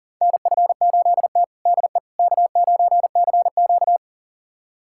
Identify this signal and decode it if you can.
Morse code, heard as a keyed tone, about 37 wpm, 700 Hz. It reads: NF8T DE K9CQ
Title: USA Callsign DE Callsign - 4 Characters - 40wpm